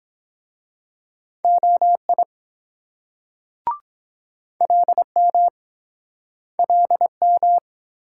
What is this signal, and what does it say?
Morse code, heard as a keyed tone, about 24 wpm, 700 Hz. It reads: OI LM LM